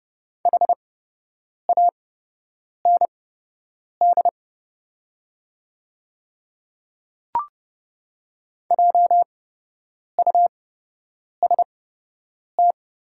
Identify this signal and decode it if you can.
Morse code, heard as a keyed tone, about 30 wpm, spaced as for 8 wpm, 700 Hz. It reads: HAND JUST
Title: 8x Character Spacing - Top 200 Words - 30wpm